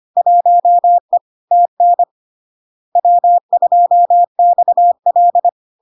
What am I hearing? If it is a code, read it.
Morse code, heard as a keyed tone, about 25 wpm, 700 Hz. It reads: 1E TN W2XL
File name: Log-in-CW-25wpm-700-Hz